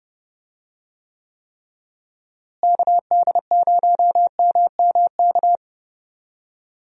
Morse code, heard as a keyed tone, about 32 wpm, 700 Hz. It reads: KD0MMK
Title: USA Call Signs - 6 Characters - Repeated 3x - 30wpm